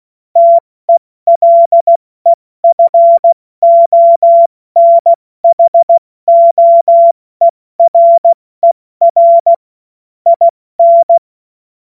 Morse code, heard as a keyed tone, about 16 wpm, spaced as for 14 wpm, 680 Hz. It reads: TELEFONHOERER IN
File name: Morsekurs / Witze / Witz 10. Mai 2022 70 F80